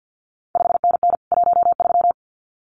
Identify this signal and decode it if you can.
Morse code, heard as a keyed tone, about 50 wpm, 700 Hz. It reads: 5NN 13
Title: CQ WW DX Contest Exchange - 50wpm